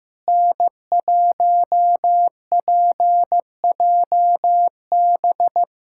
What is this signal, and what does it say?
Morse code, heard as a keyed tone, about 16 wpm, 700 Hz. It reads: N1PJB
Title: USA Callsign DE Callsign - 5 Characters - 15wpm